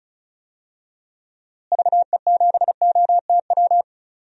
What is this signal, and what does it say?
Morse code, heard as a keyed tone, about 36 wpm, 700 Hz. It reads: VE7OTW
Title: DXCC-100 Call Signs - 6 Characters - Repeated 3x - 35wpm